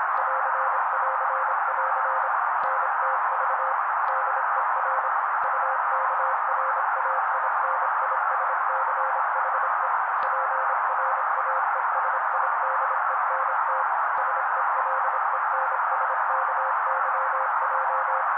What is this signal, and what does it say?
Morse code, heard as a keyed tone, about 26 wpm, 530 Hz. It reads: <AR><AR><AR> NTV DER UKRAINISCHE PRAESIDENT SELENSKYJ
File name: daily / 25wpm noise